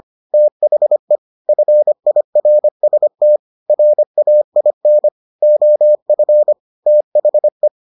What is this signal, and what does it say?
Morse code, heard as a keyed tone, about 25 wpm, 600 Hz. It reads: THE FIRST RAIN OF THE